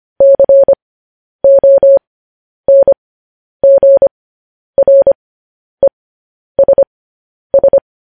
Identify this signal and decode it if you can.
Morse code, heard as a keyed tone, about 25 wpm, spaced as for 10 wpm, 550 Hz. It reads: CONGRESS
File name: podcast / News headlines-1x-25x10